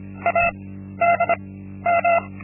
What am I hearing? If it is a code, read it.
Morse code, heard as a keyed tone, about 26 wpm, 680 Hz. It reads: A D M
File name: Linksmakalnis-2